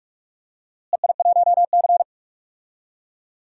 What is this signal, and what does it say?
Morse code, heard as a keyed tone, about 43 wpm, 700 Hz. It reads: EI1C
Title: DXCC-100 Call Signs - 4 Characters - Repeated 2x - 45wpm